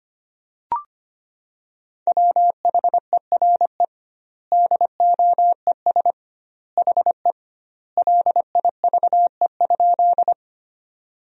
Morse code, heard as a keyed tone, about 25 wpm, 700 Hz. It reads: WHERE DOES HE LIVE?
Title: 2x Word-Spacing - Sentences from Top 200 Words - 25wpm